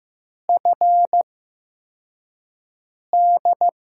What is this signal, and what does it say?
Morse code, heard as a keyed tone, about 15 wpm, spaced as for 4 wpm, 700 Hz. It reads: FD
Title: USA Call Signs - 5 Characters - 8x Character Spacing - 15wpm